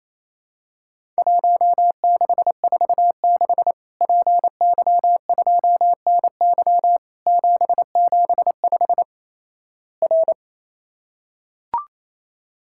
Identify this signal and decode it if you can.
Morse code, heard as a keyed tone, about 28 wpm, 700 Hz. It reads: 1646 PY2NY 775 R
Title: Worked All Europe DX Contest - QTC Exchange - 28wpm